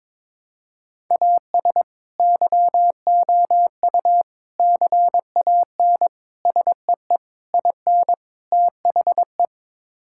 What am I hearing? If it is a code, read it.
Morse code, heard as a keyed tone, about 22 wpm, 700 Hz. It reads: AS YOU CAN SEE IN THE
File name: practice / 22wpm / Sets of 6 Words - 22wpm